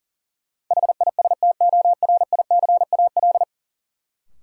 Morse code, heard as a keyed tone, about 40 wpm, 700 Hz. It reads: HISTORICAL
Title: Sentences from Top 600 Words - New Word Drill - 40wpm